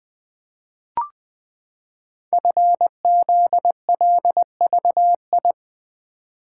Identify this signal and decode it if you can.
Morse code, heard as a keyed tone, about 20 wpm, 700 Hz. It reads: FZLVI
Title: Code Groups - 5 Letters - 20wpm